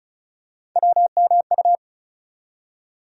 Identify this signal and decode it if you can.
Morse code, heard as a keyed tone, about 36 wpm, 700 Hz. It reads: WMU
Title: USA Call Sign Postfixes - 3 Letters 35wpm